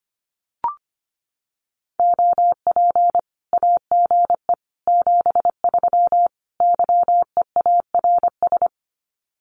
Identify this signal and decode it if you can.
Morse code, heard as a keyed tone, about 25 wpm, 700 Hz. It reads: OP AGE 73 YEARS
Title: Common QSO Phrases 25wpm